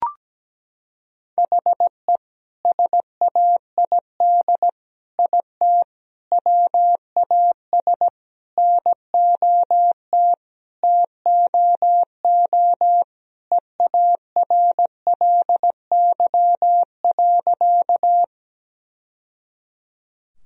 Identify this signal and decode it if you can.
Morse code, heard as a keyed tone, about 17 wpm, 700 Hz. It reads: HE SAID IT WAS NOT TOO EARLY.
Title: Sentences from Top 1000 Words - Review Part 2 - 17wpm